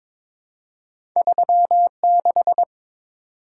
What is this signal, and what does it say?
Morse code, heard as a keyed tone, about 22 wpm, 700 Hz. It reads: 36
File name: Two Letter-Number Combinations - 22wpm